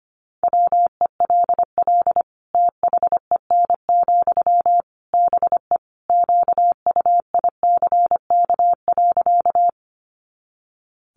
Morse code, heard as a keyed tone, about 25 wpm, 700 Hz. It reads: WELL THEN, BE QUICK.
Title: Sentences from Top 500 Words 25wpm